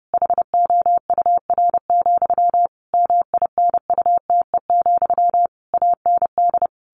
Morse code, heard as a keyed tone, about 30 wpm, 700 Hz. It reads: HOUR, MINUTE, AND
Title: A Sound Of Thunder 30wpm